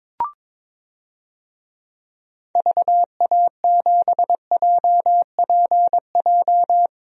Morse code, heard as a keyed tone, about 22 wpm, 700 Hz. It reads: VA7JPJ
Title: DXCC-100 Call Signs - 6 Characters - 22wpm